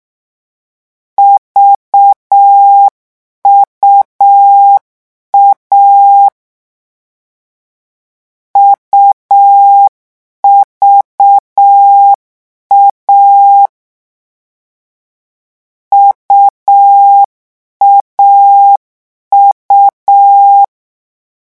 Morse code, heard as a keyed tone, about 6 wpm, 800 Hz. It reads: VUA UVA UAU
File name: morse mp3 / lecon3-vitesse 6